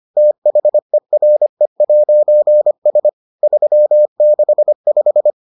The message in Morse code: THERE'S 365

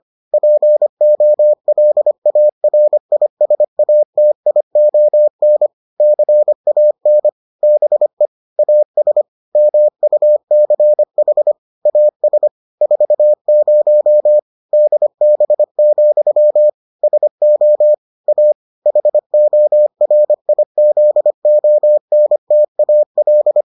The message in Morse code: POLARISATION CAN BE AS MUCH AS 40 DB, SO A HORIZONTAL

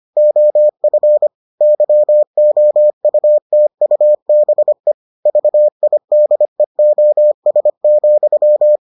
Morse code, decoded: OF YOUTUBE VIDEOS,

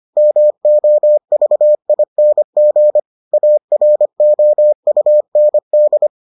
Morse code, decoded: MOVING AROUND

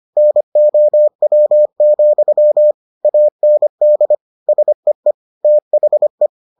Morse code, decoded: NOW, AND SEE THE

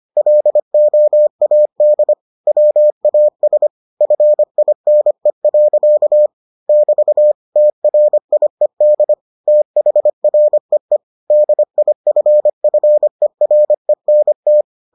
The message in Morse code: LOAD WAS FINE. <BT> TRIED THREE DIFFERENT

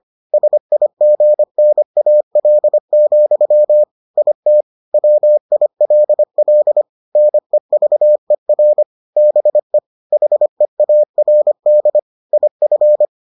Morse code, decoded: SIGNAL, IT WILL NEVER BE HEARD IF

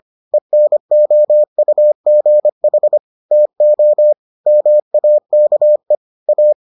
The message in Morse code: ENOUGH TO MAKE A

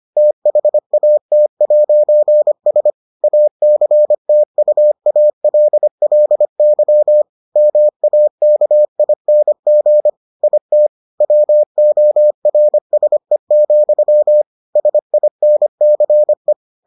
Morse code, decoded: THAT'S ACTUALLY MAKING IT WORSE, SINCE

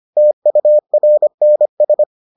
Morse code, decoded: TURNS

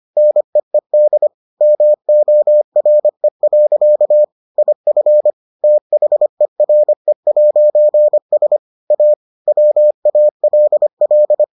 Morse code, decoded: NEED MORE. IF THERE'S A WALL

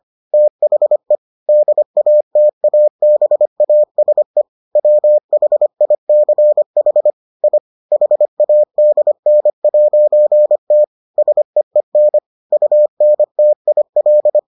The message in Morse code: THE DATABASE WHICH I HADN'T SEEN UNTIL

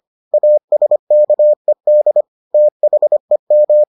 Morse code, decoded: ASKED THEM

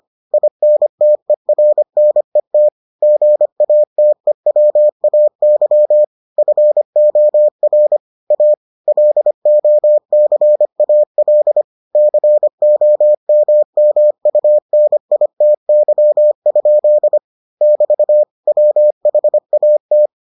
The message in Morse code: INTERNET GATEWAY FOR A LOCAL COMMUNITY? <BT> WHAT

